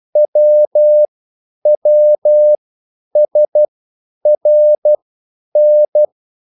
Morse code, heard as a keyed tone, about 12 wpm, spaced as for 9 wpm, 600 Hz. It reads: WWSRN